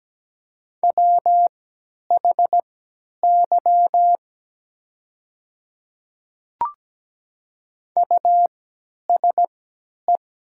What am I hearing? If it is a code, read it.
Morse code, heard as a keyed tone, about 17 wpm, spaced as for 10 wpm, 700 Hz. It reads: WHY USE
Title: 3x Character Spacing - Top 200 Words - 17wpm